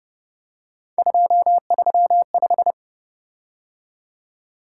Morse code, heard as a keyed tone, about 29 wpm, 700 Hz. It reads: 235
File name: Speed-Racing - Three Letter-Number Combinations - 25wpm